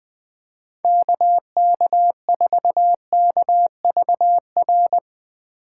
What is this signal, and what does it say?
Morse code, heard as a keyed tone, about 21 wpm, 700 Hz. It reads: KK4KVR
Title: Speed Racing - USA Call Signs 20wpm